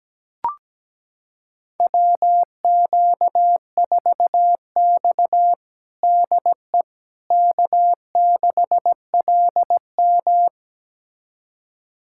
Morse code, heard as a keyed tone, about 17 wpm, 700 Hz. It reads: WQ4X DE K6LM
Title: USA Callsign DE Callsign - 4 Characters - 17wpm